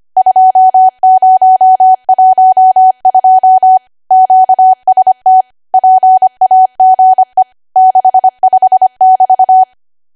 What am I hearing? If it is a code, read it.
Morse code, heard as a keyed tone, about 25 wpm, 750 Hz. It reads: 2012 QST PAGE 65 <BT>
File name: cwa / W1AW-Level3 / 25WPM